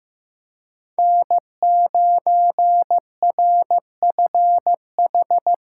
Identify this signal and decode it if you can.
Morse code, heard as a keyed tone, about 18 wpm, 700 Hz. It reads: N9RFH
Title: Speed Racing - USA Call Signs 15wpm